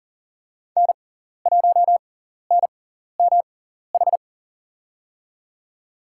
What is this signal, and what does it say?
Morse code, heard as a keyed tone, about 40 wpm, spaced as for 14 wpm, 700 Hz. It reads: N1NMH
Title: USA Call Signs - 5 Characters - 6x Character Spacing - 40wpm